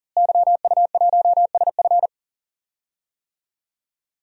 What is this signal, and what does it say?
Morse code, heard as a keyed tone, about 40 wpm, 700 Hz. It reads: YU1SF
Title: DXCC-100 Call Signs - 5 Characters - Repeated 2x - 40wpm